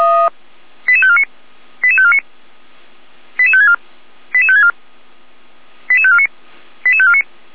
Five-tone selective call (ZVEI), heard as quick runs of five tones, tones of about 70 ms. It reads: 88549 88549 88564 88564 88549 88549